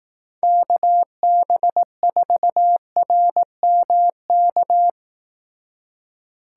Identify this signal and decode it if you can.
Morse code, heard as a keyed tone, about 18 wpm, 700 Hz. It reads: KB4RMK
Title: Speed Racing - USA 2x3 Call Signs 15wpm